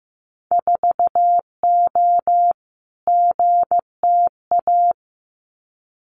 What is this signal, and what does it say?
Morse code, heard as a keyed tone, about 15 wpm, 700 Hz. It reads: 4O GTA
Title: Winter Field Day Exchanges 15wpm